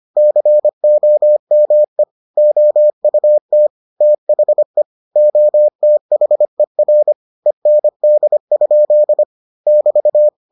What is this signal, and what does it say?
Morse code, heard as a keyed tone, about 25 wpm, 600 Hz. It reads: COME OUT THE OTHER END? <BT>